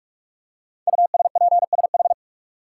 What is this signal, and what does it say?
Morse code, heard as a keyed tone, about 44 wpm, 700 Hz. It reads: USPSH
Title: Code Groups - 5 Letters - 45wpm